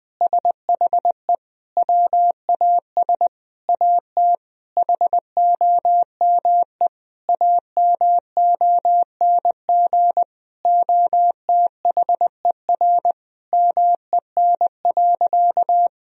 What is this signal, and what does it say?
Morse code, heard as a keyed tone, about 20 wpm, 700 Hz. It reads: SHE WAS AT HOME AMONG OTHER MEN.
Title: Sentences from Top 1000 Words - Review Part 3 - 20wpm